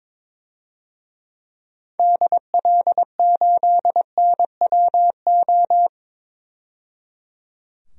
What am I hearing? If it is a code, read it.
Morse code, heard as a keyed tone, about 22 wpm, 700 Hz. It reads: DL8NWO
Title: DXCC-100 Call Signs - 6 Characters - 22wpm